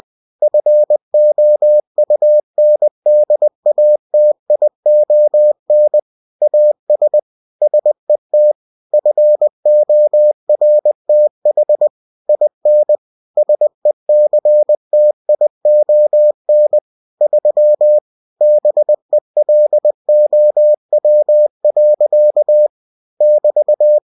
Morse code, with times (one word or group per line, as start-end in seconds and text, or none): 0.4-6.0 FOUNDATION
6.4-7.2 AS
7.6-8.5 SET
8.9-11.9 FORTH
12.3-13.0 IN
13.4-16.8 SECTION
17.2-18.0 3
18.4-22.7 BELOW.
23.2-24.0 <BT>